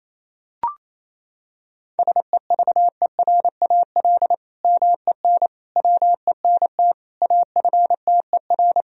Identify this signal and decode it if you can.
Morse code, heard as a keyed tone, about 28 wpm, 700 Hz. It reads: SEVERAL MEN WENT AFTER
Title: Sentences from Top 1000 Words - Review Part 2 - 28wpm